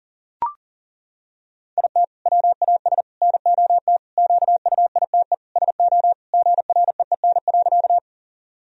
Morse code, heard as a keyed tone, about 40 wpm, 700 Hz. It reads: IT WAS NOT QUITE SO GREEN.
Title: Sentences from Top 1000 Words - Review Part 2 - 40wpm